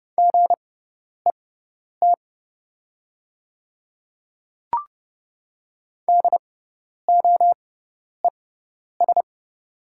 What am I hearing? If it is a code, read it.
Morse code, heard as a keyed tone, about 30 wpm, spaced as for 10 wpm, 700 Hz. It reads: GET DOES
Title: 6x Character Spacing - Top 200 Words - 30wpm